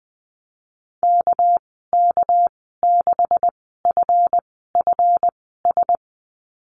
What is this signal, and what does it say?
Morse code, heard as a keyed tone, about 20 wpm, spaced as for 14 wpm, 700 Hz. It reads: KK6FFS
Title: 238 Lesson 38g - Letter q - Call Signs 20wpm